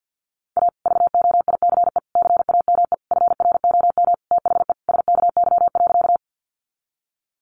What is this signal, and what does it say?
Morse code, heard as a keyed tone, about 50 wpm, 700 Hz. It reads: A VOICE CAME FROM THE SKY.
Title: Sentences from Top 1200 Words 50wpm